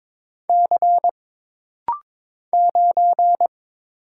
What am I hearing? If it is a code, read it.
Morse code, heard as a keyed tone, about 22 wpm, 700 Hz. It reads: C 9
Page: This rapid-fire timing is geared to push your ICR (Instant Character Recognition) skills.